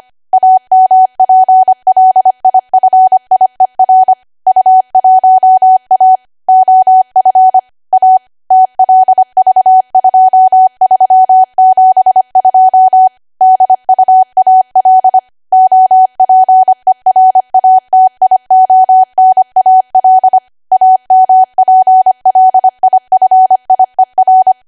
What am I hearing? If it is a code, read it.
Morse code, heard as a keyed tone, about 25 wpm, 750 Hz. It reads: AMPLIFIER U1A OF A TLV2372 DUAL OPERATIONAL AMPLIFIER
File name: Archive / 25 WPM / 25WPM